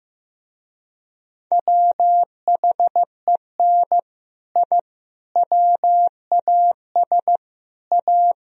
Morse code, heard as a keyed tone, about 15 wpm, 700 Hz. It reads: WHEN I WAS A
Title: Speed-Racing - Sets of 5 Words 15wpm